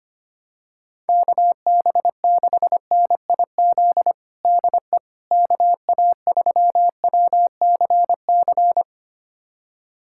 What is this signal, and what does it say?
Morse code, heard as a keyed tone, about 25 wpm, 700 Hz. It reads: KB6NIZ DE KA3WCC
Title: USA Callsign DE Callsign - 6 Characters - 25wpm